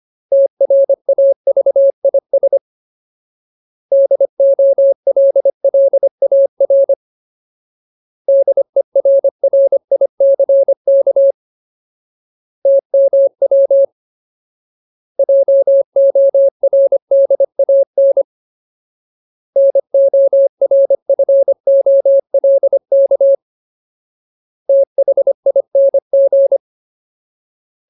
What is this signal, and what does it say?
Morse code, heard as a keyed tone, about 25 wpm, 550 Hz. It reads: TRAVIS DOLLAR DERRICK TMW JORDAN NORFOLK THING
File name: Ultimate Word List / 25WPM Ultimate